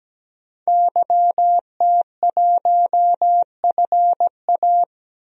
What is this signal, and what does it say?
Morse code, heard as a keyed tone, about 17 wpm, 700 Hz. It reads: YT1FA
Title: DXCC-100 Call Signs - 5 Characters - 17wpm